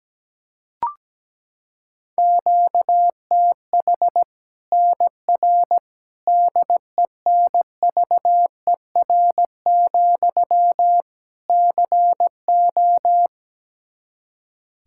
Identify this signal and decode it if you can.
Morse code, heard as a keyed tone, about 17 wpm, 700 Hz. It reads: QTH NR DENVER, CO
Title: K1USN SST QSO Practice 17wpm